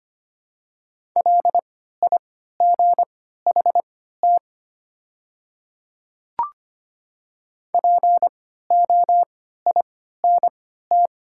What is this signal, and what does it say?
Morse code, heard as a keyed tone, about 25 wpm, spaced as for 14 wpm, 700 Hz. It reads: LIGHT POINT
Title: 3x Character Spacing - Top 300 Words - 25wpm